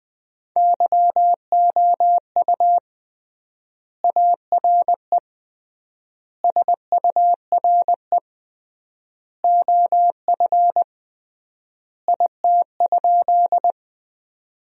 Morse code, heard as a keyed tone, about 20 wpm, 700 Hz. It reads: YOU ARE SURE OF IT?
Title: 3x Word-Spacing - Sentences from Top 300 Words - 20wpm